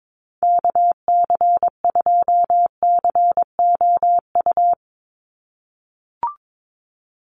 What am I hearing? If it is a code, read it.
Morse code, heard as a keyed tone, about 22 wpm, 700 Hz. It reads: KC2COU
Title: USA Call Signs 22wpm